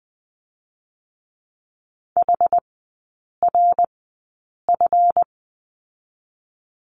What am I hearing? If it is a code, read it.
Morse code, heard as a keyed tone, about 20 wpm, spaced as for 8 wpm, 700 Hz. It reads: HRF
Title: 138 Lesson 23e - Letter f - Three Characters 20wpm